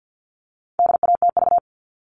Morse code, heard as a keyed tone, about 50 wpm, 700 Hz. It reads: DATV
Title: All Cut Numbers - 4-Digit 50wpm